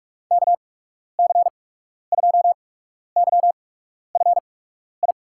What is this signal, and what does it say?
Morse code, heard as a keyed tone, about 45 wpm, spaced as for 12 wpm, 700 Hz. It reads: KC2YFI
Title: USA Call Signs - 6 Characters - 8x Character Spacing - 45wpm